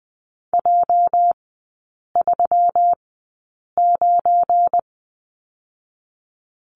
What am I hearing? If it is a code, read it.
Morse code, heard as a keyed tone, about 20 wpm, spaced as for 8 wpm, 700 Hz. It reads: J39
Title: 205 Lesson 33e - Letter j - Three Characters 20wpm